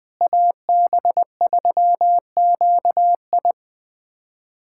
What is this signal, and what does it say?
Morse code, heard as a keyed tone, about 20 wpm, 700 Hz. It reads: AB3QI
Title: USA Callsign DE Callsign - 5 Characters - 20wpm